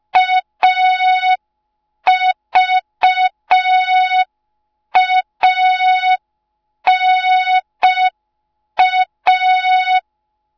Morse code, heard as a keyed tone, about 5 wpm, 3700 Hz. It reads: AVANA